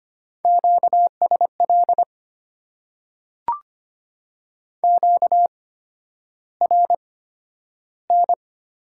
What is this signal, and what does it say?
Morse code, heard as a keyed tone, about 25 wpm, spaced as for 11 wpm, 700 Hz. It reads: QSL QRN
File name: Top 90 QSO Elements - 8x Character Spacing - 25wpm